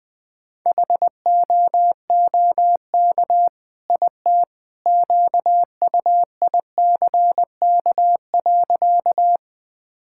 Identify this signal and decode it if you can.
Morse code, heard as a keyed tone, about 20 wpm, 700 Hz. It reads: HOOK IT QUICK.
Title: Sentences from Top 1600 Words 20wpm